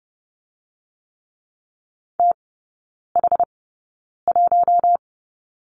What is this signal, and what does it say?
Morse code, heard as a keyed tone, about 30 wpm, spaced as for 9 wpm, 700 Hz. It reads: TH1
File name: Lesson 15e - Number 5 - Three Characters 30wpm